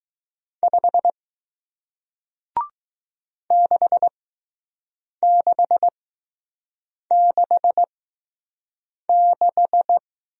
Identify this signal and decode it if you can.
Morse code, heard as a keyed tone, about 20 wpm, spaced as for 6 wpm, 700 Hz. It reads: 5 6666